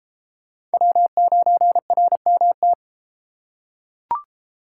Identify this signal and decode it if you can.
Morse code, heard as a keyed tone, about 31 wpm, 700 Hz. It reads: W9RMT